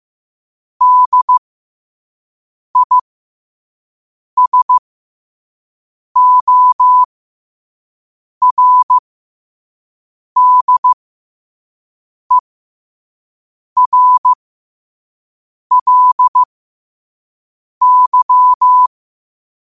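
Morse code, heard as a keyed tone, about 15 wpm, spaced as for 5 wpm, 1000 Hz. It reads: DISORDERLY
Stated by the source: Morse code sent at 5 wpm.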